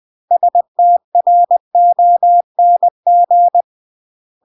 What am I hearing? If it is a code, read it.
Morse code, heard as a keyed tone, about 20 wpm, 700 Hz. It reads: STRONG
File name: CW / jodorowsky / 82 commandments 20-20